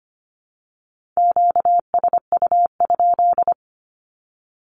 Morse code, heard as a keyed tone, about 25 wpm, 700 Hz. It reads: QSU?
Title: Q-Codes with Spoken Meaning - 25wpm